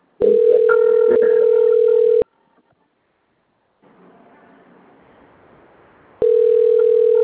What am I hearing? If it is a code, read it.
Ringback tone (North America), 440 + 480 Hz.
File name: out- - - - - .96675